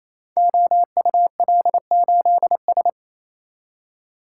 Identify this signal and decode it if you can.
Morse code, heard as a keyed tone, about 28 wpm, 700 Hz. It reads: OUL8S
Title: Code Groups - 5 Letter:Numbers - 28wpm